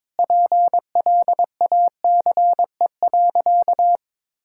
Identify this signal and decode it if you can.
Morse code, heard as a keyed tone, about 22 wpm, 700 Hz. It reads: PLACE.
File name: practice / 22wpm / Sentences with 8-Word Phrases - 22wpm